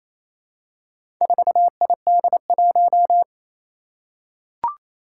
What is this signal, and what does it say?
Morse code, heard as a keyed tone, about 27 wpm, 700 Hz. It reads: 4ID1
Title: Four Challenging Letter-Number Combinations 28wpm